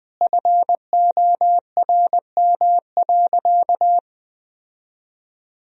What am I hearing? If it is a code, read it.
Morse code, heard as a keyed tone, about 20 wpm, 700 Hz. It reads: FORM.
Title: Sentences from Top 1000 Words - Review Part 1 - 20wpm